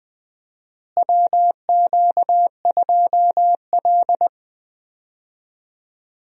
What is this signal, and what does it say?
Morse code, heard as a keyed tone, about 20 wpm, 700 Hz. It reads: WQ2L
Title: USA Call Signs - 4 Characters - Repeated 2x - 20wpm